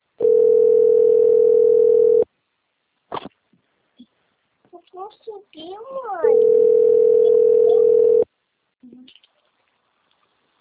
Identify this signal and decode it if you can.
Ringback tone (North America), 440 + 480 Hz, with speech.
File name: rg-1600-unknown- - - .58910